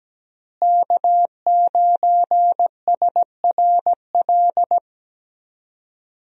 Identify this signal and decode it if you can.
Morse code, heard as a keyed tone, about 17 wpm, 700 Hz. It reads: K9SRL
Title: USA Call Signs - 5 Characters - Repeated 2x - 17wpm